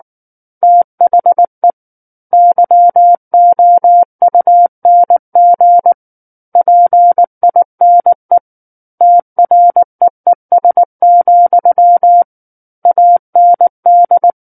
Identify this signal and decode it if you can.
Morse code, heard as a keyed tone, about 19 wpm, 700 Hz. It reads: THE YOUNG PINE TREES, AND
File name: Book One / 10-Chapter 10 - 19 WPM